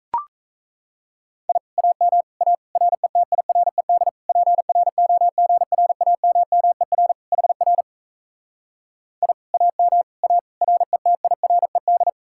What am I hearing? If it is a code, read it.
Morse code, heard as a keyed tone, about 37 wpm, 700 Hz. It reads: I AM A RETIRED PROGRAMMER HR I AM A RETIRED